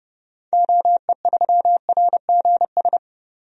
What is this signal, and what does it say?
Morse code, heard as a keyed tone, about 32 wpm, 700 Hz. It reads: OE3RGS
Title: DXCC-100 Call Signs - 6 Characters - 30wpm